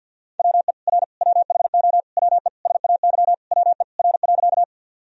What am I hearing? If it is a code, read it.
Morse code, heard as a keyed tone, about 52 wpm, 700 Hz. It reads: WE R WHO WE SAY WE R.
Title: Sentences from Top 200 Words and Abbreviations 50wpm